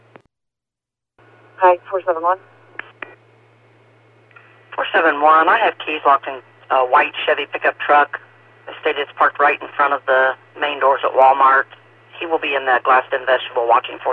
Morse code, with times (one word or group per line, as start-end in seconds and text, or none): none